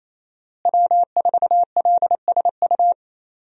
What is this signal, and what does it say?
Morse code, heard as a keyed tone, about 28 wpm, 700 Hz. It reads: W4LSU
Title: USA Call Signs - 5 Characters - Repeated 3x - 28wpm